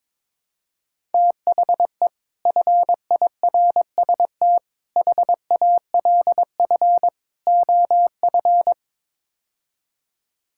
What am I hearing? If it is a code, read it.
Morse code, heard as a keyed tone, about 22 wpm, 700 Hz. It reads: THE FIRST HALF OF
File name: practice / 22wpm / Build A Phrase - 5-Word Phrases - 22wpm